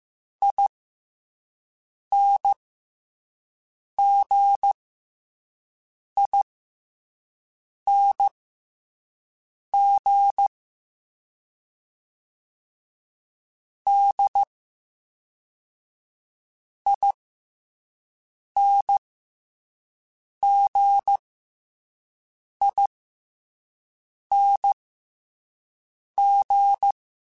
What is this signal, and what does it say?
Morse code, heard as a keyed tone, about 15 wpm, spaced as for 5 wpm, 800 Hz. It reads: INGING DINGING